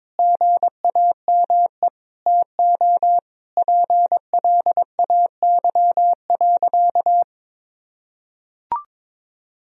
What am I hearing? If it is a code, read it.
Morse code, heard as a keyed tone, about 22 wpm, 700 Hz. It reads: GAME TO PLAY.
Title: Sentences from Top 1000 Words - Review Part 3 - 22wpm